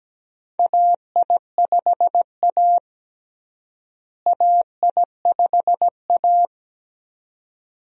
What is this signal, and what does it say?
Morse code, heard as a keyed tone, about 17 wpm, 700 Hz. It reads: AI5A AI5A
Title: USA Call Signs - 4 Characters - Repeated 2x - 17wpm